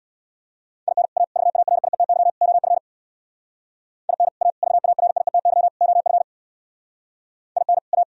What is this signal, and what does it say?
Morse code, heard as a keyed tone, about 60 wpm, 700 Hz. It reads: IN A VARIETY OF IN A VARIETY OF IN A
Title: Speed-Racing - Sets of 4 Words 50wpm